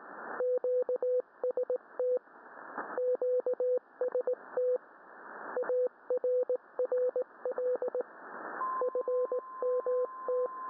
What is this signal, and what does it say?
Morse code, heard as a keyed tone, about 19 wpm, 500 Hz. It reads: QST QST ARRL FMT